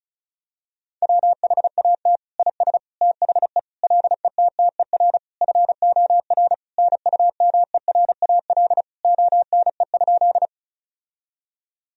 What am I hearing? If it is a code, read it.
Morse code, heard as a keyed tone, about 36 wpm, 700 Hz. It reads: WHAT IS THE LETTER FOR NUMERAL ONE?
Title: Sentences from Top 1000 Words - Review Part 2 - 35wpm